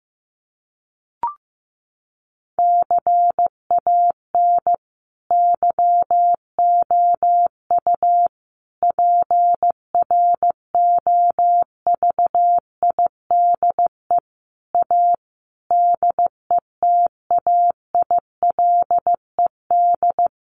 Morse code, heard as a keyed tone, about 15 wpm, 700 Hz. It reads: CAN YOU PROVIDE A DETAILED
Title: Sentences from Top 1500 Words 15wpm